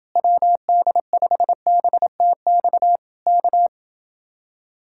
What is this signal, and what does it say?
Morse code, heard as a keyed tone, about 30 wpm, 700 Hz. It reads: WD5BTX K